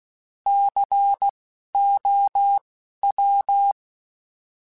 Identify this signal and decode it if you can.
Morse code, heard as a keyed tone, about 16 wpm, spaced as for 12 wpm, 800 Hz. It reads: COW
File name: Alphabetical / Alphabetical mp3s / Cs @10